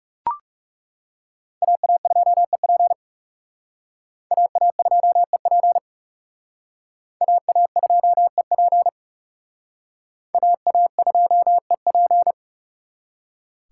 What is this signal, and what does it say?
Morse code, heard as a keyed tone, about 37 wpm, 700 Hz. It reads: AA2EP AA2EP AA2EP AA2EP
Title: Speed Racing - USA Call Signs 30wpm